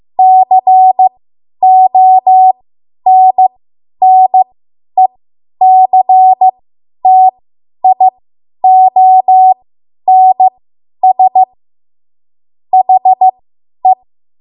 Morse code, heard as a keyed tone, about 15 wpm, spaced as for 10 wpm, 750 Hz. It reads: CONNECTIONS HE
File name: Morse / 10WPM